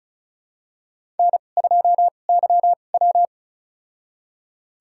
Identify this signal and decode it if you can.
Morse code, heard as a keyed tone, about 36 wpm, spaced as for 25 wpm, 700 Hz. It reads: N2YW
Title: USA Call Signs - 4 Characters - 2x Character Spacing - 35wpm